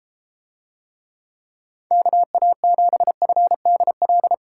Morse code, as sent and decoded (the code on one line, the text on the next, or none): -.- .- --... ..-. -.. .-..
KA7FDL